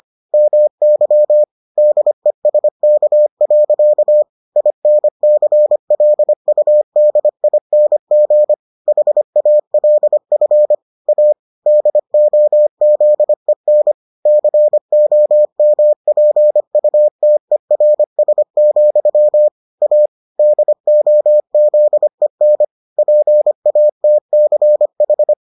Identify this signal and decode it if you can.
Morse code, heard as a keyed tone, about 25 wpm, 600 Hz. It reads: MY DESK. INCLUDING HALF A DOZEN COMPUTERS, A DOZEN PATCH